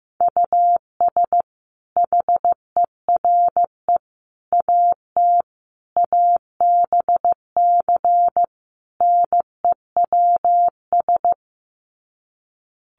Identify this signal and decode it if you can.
Morse code, heard as a keyed tone, about 15 wpm, 700 Hz. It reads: US HERE AT ABC NEWS
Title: Sets of 5 Words 15wpm